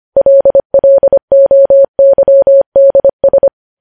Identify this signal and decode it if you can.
Morse code, heard as a keyed tone, about 25 wpm, 550 Hz. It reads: LLOYDS
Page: news headlines in morse code at 25 wpm